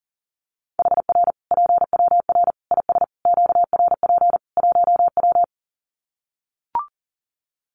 Morse code, heard as a keyed tone, about 40 wpm, 700 Hz. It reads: HR PWR IS QRP 1W
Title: Common QSO Phrases 40wpm